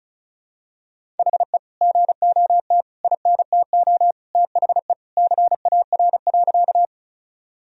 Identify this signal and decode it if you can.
Morse code, heard as a keyed tone, about 36 wpm, 700 Hz. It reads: HE GOT INTO THE CAR.
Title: Sentences from Top 1000 Words - Review Part 1 - 35wpm